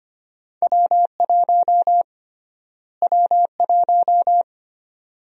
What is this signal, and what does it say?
Morse code, heard as a keyed tone, about 25 wpm, 700 Hz. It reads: W1 W1
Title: USA Call Sign Prefixes - Repeated 2x - 25wpm